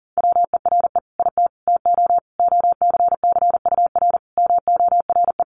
Morse code, heard as a keyed tone, about 40 wpm, 700 Hz. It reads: WERE IT TO OCCUR MORE
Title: Sentences from Top 1300 Words 40wpm